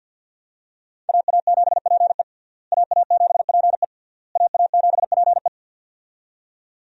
Morse code, heard as a keyed tone, about 53 wpm, 700 Hz. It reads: AA7PE AA7PE AA7PE
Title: USA Call Signs - 5 Characters - Repeated 3x - 50wpm